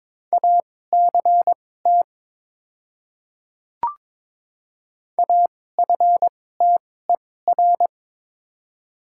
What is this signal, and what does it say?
Morse code, heard as a keyed tone, about 21 wpm, spaced as for 16 wpm, 700 Hz. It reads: ACT AFTER
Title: 2x Character Spacing - Top 200 Words - 22wpm